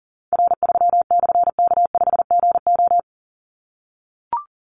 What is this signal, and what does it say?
Morse code, heard as a keyed tone, about 38 wpm, 700 Hz. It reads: R3/K5GO
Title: Call Signs - 40wpm